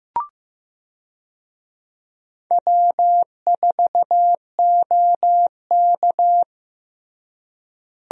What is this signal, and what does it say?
Morse code, heard as a keyed tone, about 15 wpm, 700 Hz. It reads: W4OK
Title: USA Call Signs - 4 Characters - Repeated 3x - 15wpm